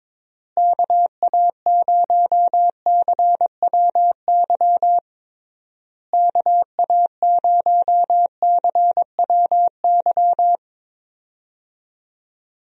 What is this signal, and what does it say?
Morse code, heard as a keyed tone, about 21 wpm, 700 Hz. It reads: KA0CWY KA0CWY
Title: USA Call Signs - 6 Characters - Repeated 3x - 22wpm